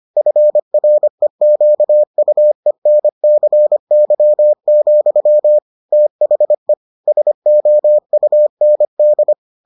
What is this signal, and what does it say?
Morse code, heard as a keyed tone, about 25 wpm, 600 Hz. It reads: FREQUENCY, THE SOUND